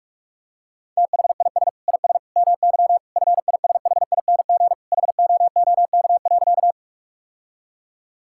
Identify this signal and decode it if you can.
Morse code, heard as a keyed tone, about 43 wpm, 700 Hz. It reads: THIS IS MY FISHING HOOK.
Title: Sentences from Top 1600 Words 45wpm